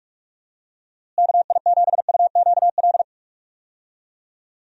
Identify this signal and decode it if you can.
Morse code, heard as a keyed tone, about 48 wpm, 700 Hz. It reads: KI7UQL
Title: USA Call Signs - 6 Characters - Repeated 2x - 45wpm